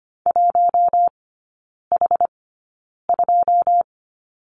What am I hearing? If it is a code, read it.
Morse code, heard as a keyed tone, about 25 wpm, spaced as for 9 wpm, 690 Hz. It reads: 1H2
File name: Lesson 14e - Number 2 - Three Characters 25wpm